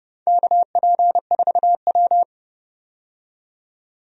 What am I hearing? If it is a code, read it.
Morse code, heard as a keyed tone, about 31 wpm, 700 Hz. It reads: KP4W
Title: DXCC-100 Call Signs - 4 Characters - 30wpm